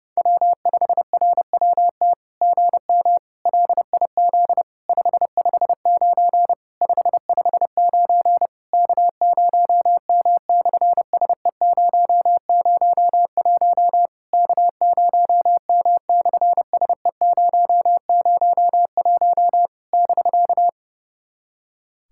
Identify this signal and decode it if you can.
Morse code, heard as a keyed tone, about 30 wpm, 700 Hz. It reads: W5RWT GM LIZ 559 559 K0M/SE001 K0M/SE001 <BK>
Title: SOTA QSO Practice 30wpm